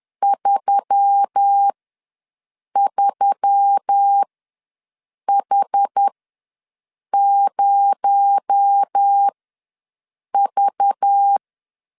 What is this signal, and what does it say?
Morse code, heard as a keyed tone, about 10 wpm, spaced as for 6 wpm, 800 Hz. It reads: 33H0V
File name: LEZIONI / 8 YJ45 / 8 Y ws 10wpm 3952